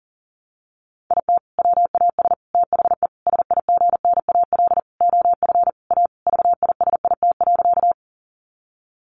Morse code, heard as a keyed tone, about 40 wpm, 700 Hz. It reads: IT WAS THE SIGNAL OF A VISIT.
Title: Sentences from Top 1400 Words 40wpm